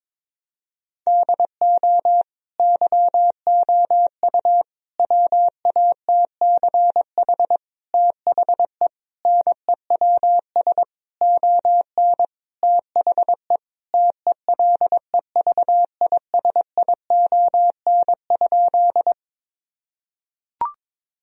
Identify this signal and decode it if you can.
Morse code, heard as a keyed tone, about 21 wpm, 700 Hz. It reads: DO YOU WATCH THE NEWS ON THE TELEVISION?
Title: Sentences from Top 1000 Words - Review Part 4 - 22wpm